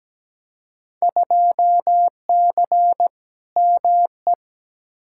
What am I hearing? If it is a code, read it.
Morse code, heard as a keyed tone, about 17 wpm, 700 Hz. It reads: 2C ME
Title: Field Day Exchanges Encore - 17wpm